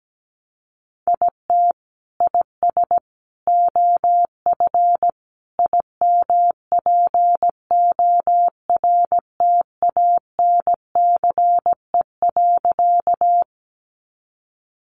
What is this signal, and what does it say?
Morse code, heard as a keyed tone, about 17 wpm, 700 Hz. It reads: IT IS OF IMPORTANCE.
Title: Sentences from Top 1200 Words 17wpm